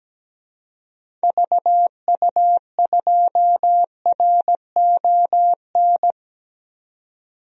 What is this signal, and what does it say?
Morse code, heard as a keyed tone, about 17 wpm, 700 Hz. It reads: VU2RON
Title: DXCC-100 Call Signs - 6 Characters - 17wpm